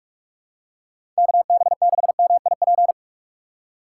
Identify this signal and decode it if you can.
Morse code, heard as a keyed tone, about 46 wpm, 700 Hz. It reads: KB6MIP
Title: USA Call Signs - 6x Character Spacing - 45wpm